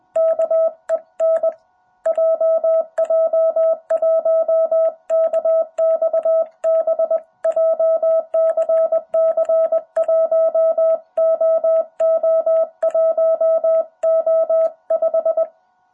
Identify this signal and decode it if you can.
Morse code, heard as a keyed tone, about 20 wpm, 660 Hz. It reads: KEN JJ1KXB JCC 1OO1O5
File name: jh8cbh / 2021cw